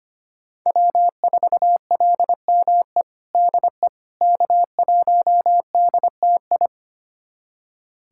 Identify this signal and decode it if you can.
Morse code, heard as a keyed tone, about 25 wpm, 700 Hz. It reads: W4LME DE K1DTI
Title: USA Callsign DE Callsign - 5 Characters - 25wpm